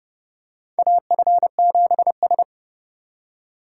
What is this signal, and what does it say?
Morse code, heard as a keyed tone, about 32 wpm, 700 Hz. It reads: AF7S
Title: USA Call Signs - 4 Characters - 2x Character Spacing - 30wpm